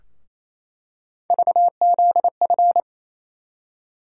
Morse code, heard as a keyed tone, about 28 wpm, 700 Hz. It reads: VZF
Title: USA Call Sign Postfixes - 3 Letters 28wpm